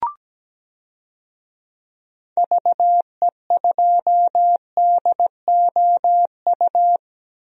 Morse code, heard as a keyed tone, about 17 wpm, 700 Hz. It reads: VE2DOU
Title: DXCC-100 Call Signs - 6 Characters - 17wpm